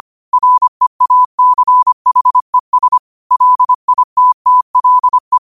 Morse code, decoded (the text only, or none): REACHES LITTLE